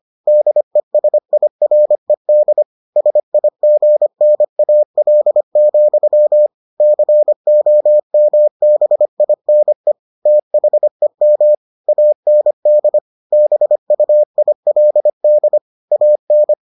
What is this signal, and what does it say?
Morse code, heard as a keyed tone, about 25 wpm, 600 Hz. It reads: DESIRED SIGNAL, COMBINE THEM AND BUILD AN